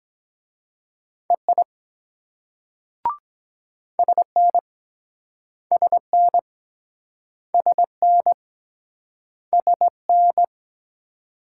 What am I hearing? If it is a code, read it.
Morse code, heard as a keyed tone, about 22 wpm, 700 Hz. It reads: EI SN SN SN SN